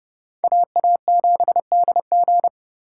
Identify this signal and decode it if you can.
Morse code, heard as a keyed tone, about 29 wpm, 700 Hz. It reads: AA7DG